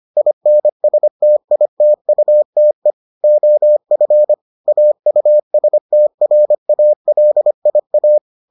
Morse code, heard as a keyed tone, about 25 wpm, 600 Hz. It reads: INSTITUTE OF AUSTRALIA